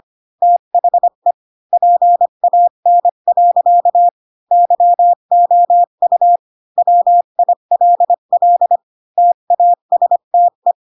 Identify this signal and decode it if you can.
Morse code, heard as a keyed tone, about 25 wpm, 700 Hz. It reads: THE PAN. YOU WILL TASTE
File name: Pinocchio / Pinocchio ch5 20 25 ik1hge